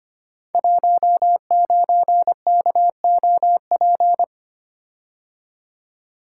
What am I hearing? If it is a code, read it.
Morse code, heard as a keyed tone, about 25 wpm, 700 Hz. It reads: 19KOP
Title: Code Groups - 5 Letter:Numbers - 25wpm